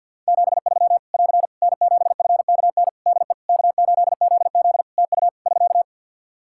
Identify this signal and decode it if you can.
Morse code, heard as a keyed tone, about 47 wpm, 700 Hz. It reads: 73 <AR> N7FKN DE K8ZZ TU <SK>
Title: Practice QSO - Exchange 4 - Ending - 50wpm